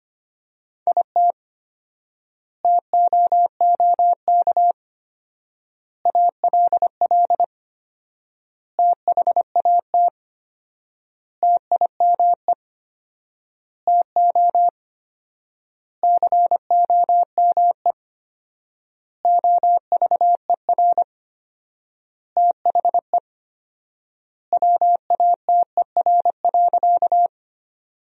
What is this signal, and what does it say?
Morse code, heard as a keyed tone, about 25 wpm, 700 Hz. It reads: IT TOOK ALL THAT TIME TO COME OVER THE WATER.
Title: 4x Word-Spacing - Sentences from Top 300 Words - 25wpm